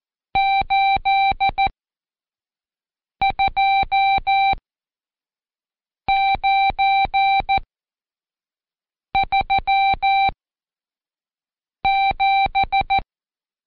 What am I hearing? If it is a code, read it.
Morse code, heard as a keyed tone, about 14 wpm, spaced as for 5 wpm, 780 Hz. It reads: 82937
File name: Code Quick / AudioDownld2 / Seg91-Day19-Review Numb